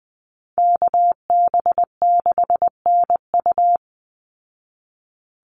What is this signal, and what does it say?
Morse code, heard as a keyed tone, about 20 wpm, 700 Hz. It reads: KB6NU
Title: Call Signs Repeated 2x - 20wpm